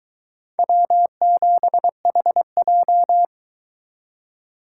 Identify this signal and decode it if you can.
Morse code, heard as a keyed tone, about 23 wpm, 700 Hz. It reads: W7HJ